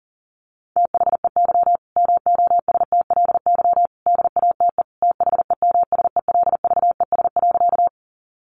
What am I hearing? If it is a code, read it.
Morse code, heard as a keyed tone, about 40 wpm, 700 Hz. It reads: THEY MOSTLY DATE THEMSELVES.
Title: Sentences from Top 1300 Words 40wpm